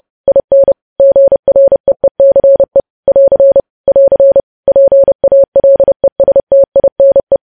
Morse code, heard as a keyed tone, about 30 wpm, 550 Hz. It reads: IN GREECE <AR> <AR> PALESTINE
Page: news headlines in morse code at 30 wpm